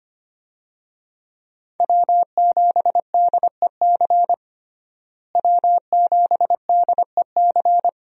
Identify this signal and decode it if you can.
Morse code, heard as a keyed tone, about 25 wpm, 700 Hz. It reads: W7DEC W7DEC
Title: USA Call Signs - 5 Characters - Repeated 2x - 25wpm